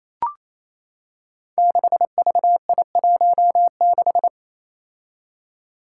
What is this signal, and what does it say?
Morse code, heard as a keyed tone, about 28 wpm, 700 Hz. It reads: 6VI16
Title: Five Challenging Letter-Number Combinations 28wpm